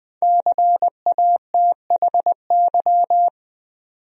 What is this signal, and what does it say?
Morse code, heard as a keyed tone, about 20 wpm, 700 Hz. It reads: CATHY
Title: USA Common Names - 20wpm